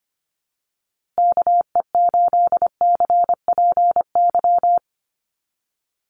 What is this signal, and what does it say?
Morse code, heard as a keyed tone, about 24 wpm, 700 Hz. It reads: KE8CPY
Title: USA Call Signs Repeated 3x 25wpm